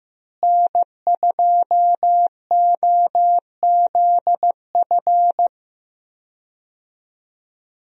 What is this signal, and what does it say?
Morse code, heard as a keyed tone, about 16 wpm, 700 Hz. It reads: N2OZF
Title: USA Call Signs - 5 Characters - Repeated 2x - 15wpm